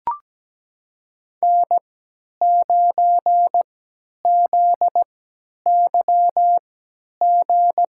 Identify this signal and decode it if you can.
Morse code, heard as a keyed tone, about 17 wpm, spaced as for 10 wpm, 700 Hz. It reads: N9ZYG
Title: USA Call Signs - 5 Characters - 3x Character Spacing - 17wpm